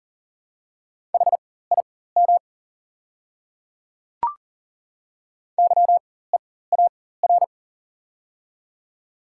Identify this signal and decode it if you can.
Morse code, heard as a keyed tone, about 40 wpm, spaced as for 19 wpm, 700 Hz. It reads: HIM YEAR
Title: 4x Character Spacing - Top 200 Words - 40wpm